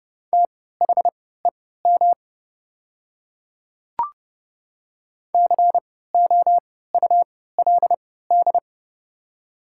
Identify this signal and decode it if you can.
Morse code, heard as a keyed tone, about 31 wpm, spaced as for 17 wpm, 690 Hz. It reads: THEM COULD
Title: 3x Character Spacing - Top 200 Words - 30wpm